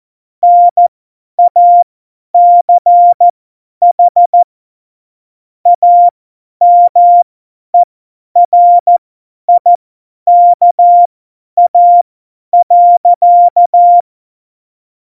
Morse code, heard as a keyed tone, about 14 wpm, spaced as for 10 wpm, 700 Hz. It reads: NACH AMERIKA.
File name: Morsekurs / Witze / Witz 22. Dezember 2020, 50 F70